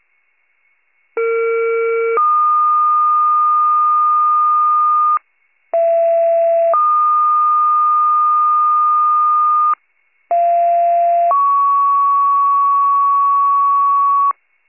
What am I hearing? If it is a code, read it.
Two-tone page, heard as a one-second tone, then a three-second tone, about 460 then 1220 Hz, about 670 then 1220 Hz, about 690 then 1120 Hz. It is heard from 1.2 to 14.3 s.
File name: pager / 39 RESCUE / wcalert